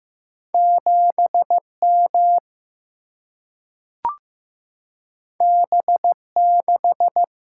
Morse code, heard as a keyed tone, about 15 wpm, 700 Hz. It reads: 7M B6
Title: Single-Letter Single-Number 15wpm